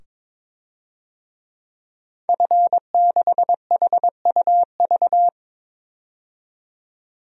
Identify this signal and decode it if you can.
Morse code, heard as a keyed tone, about 22 wpm, 700 Hz. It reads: F6HUV
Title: DXCC-100 Call Signs - 5 Characters - 22wpm